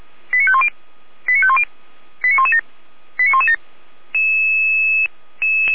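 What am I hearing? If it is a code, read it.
Five-tone selective call (ZVEI), heard as quick runs of five tones, tones of about 70 ms. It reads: 89510 89510 89117 89117
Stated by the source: ZVEI five-tone selective call 89115 received by SDR.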